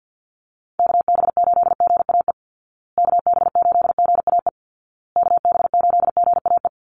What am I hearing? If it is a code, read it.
Morse code, heard as a keyed tone, about 50 wpm, 700 Hz. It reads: KB8GAE KB8GAE KB8GAE
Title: Call Signs Repeated 2x - 50wpm